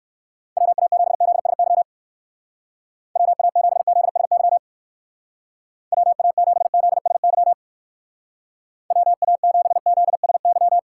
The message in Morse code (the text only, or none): WA7ZSY WA7ZSY WA7ZSY WA7ZSY